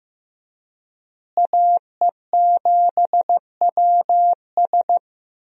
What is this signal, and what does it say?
Morse code, heard as a keyed tone, about 16 wpm, 700 Hz. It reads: AE7WS
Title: USA Call Signs - 5 Characters - Repeated 2x - 15wpm